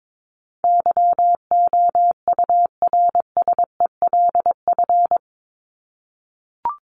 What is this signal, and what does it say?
Morse code, heard as a keyed tone, about 22 wpm, 700 Hz. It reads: YOURSELF
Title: Top 501-750 Words 22wpm